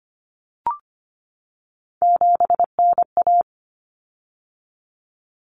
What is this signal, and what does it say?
Morse code, heard as a keyed tone, about 25 wpm, 700 Hz. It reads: 7NA
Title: Common Cut Numbers - 3-Digit 25wpm